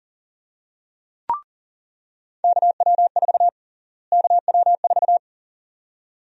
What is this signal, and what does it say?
Morse code, heard as a keyed tone, about 40 wpm, 700 Hz. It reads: KW4 KW4
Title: USA Call Sign Prefixes - Repeated 2x - 40wpm